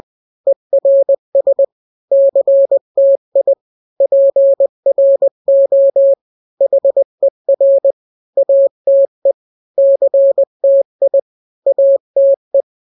Morse code, heard as a keyed tone, about 20 wpm, 550 Hz. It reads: ERS CTI PRO HER ATE CTI ATE
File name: n-grams / 3-grams 20 18 WPM